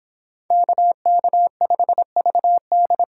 Morse code, heard as a keyed tone, about 30 wpm, 700 Hz. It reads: KK5VD